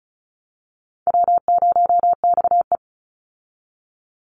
Morse code, heard as a keyed tone, about 34 wpm, 700 Hz. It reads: W0XE
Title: Easy Call Signs Repeated 2x - 35wpm